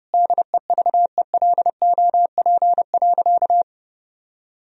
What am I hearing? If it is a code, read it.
Morse code, heard as a keyed tone, about 30 wpm, 700 Hz. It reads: DEVELOP.
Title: Sentences from Top 1000 Words - Review Part 3 - 30wpm